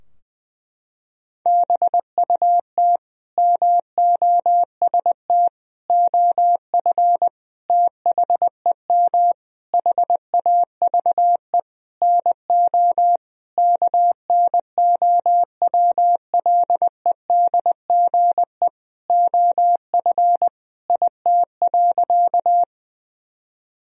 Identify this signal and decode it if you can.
Morse code, heard as a keyed tone, about 21 wpm, 700 Hz. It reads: BUT MOST OF THEM HAVE NO KNOWLEDGE OF IT.
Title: Sentences from Top 1000 Words - Review Part 3 - 20wpm